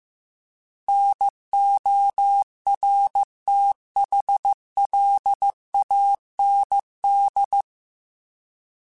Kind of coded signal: Morse code